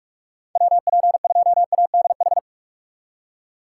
Morse code, heard as a keyed tone, about 44 wpm, 700 Hz. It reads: WP2ADH
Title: USA Call Signs - 3x Character Spacing - 45wpm